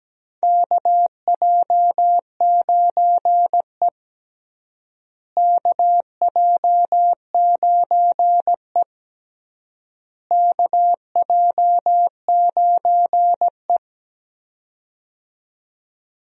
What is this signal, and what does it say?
Morse code, heard as a keyed tone, about 17 wpm, 700 Hz. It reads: KJ9E KJ9E KJ9E
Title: USA Call Signs - 4 Characters - Repeated 3x - 17wpm